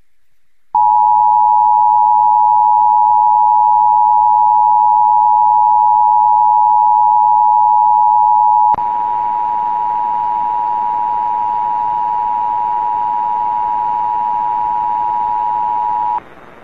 Emergency Alert System (SAME) tones (the 853 and 960 Hz attention tone). It is heard from 1.1 to 16.2 s.